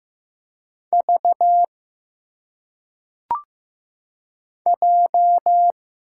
Morse code, heard as a keyed tone, about 15 wpm, 700 Hz. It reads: V J
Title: Single Letters - Phonetic Pronunciation 15wpm